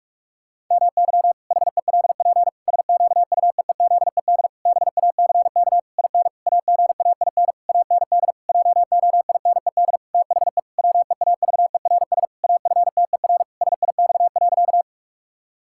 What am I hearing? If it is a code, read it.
Morse code, heard as a keyed tone, about 46 wpm, 700 Hz. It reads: MY HELP SQUEEZED BACK IN AGAIN AND JOINED THE WEAVERS AFTER SIX.
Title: Pangram Challenge v2 - 45wpm